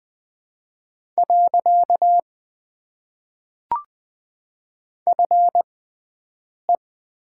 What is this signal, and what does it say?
Morse code, heard as a keyed tone, about 20 wpm, spaced as for 7 wpm, 700 Hz. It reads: . FE
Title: Top 90 QSO Elements - 6x Character Spacing - 20wpm